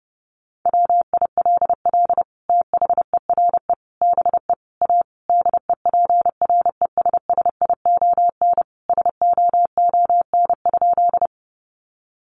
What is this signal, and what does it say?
Morse code, heard as a keyed tone, about 30 wpm, 700 Hz. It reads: WILL THERE BE A DEPRESSION SOON?
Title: Sentences from Top 1300 Words 30wpm